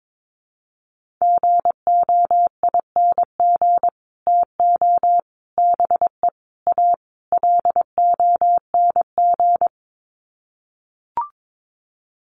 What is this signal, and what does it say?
Morse code, heard as a keyed tone, about 22 wpm, 700 Hz. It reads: GOING TO BE A LONG
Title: Sets of 5 Words Encore2 22wpm